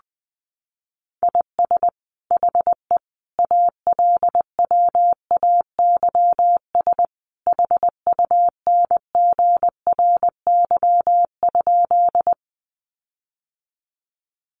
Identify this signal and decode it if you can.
Morse code, heard as a keyed tone, about 20 wpm, 700 Hz. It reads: IS HE ALWAYS HUNGRY?
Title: Sentences from Top 1400 Words 20wpm